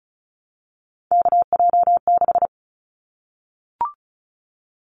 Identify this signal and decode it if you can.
Morse code, heard as a keyed tone, about 34 wpm, 700 Hz. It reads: KJ6
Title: Callsign Prefixes for Top 25 DXCC - Repeated 3x - 35wpm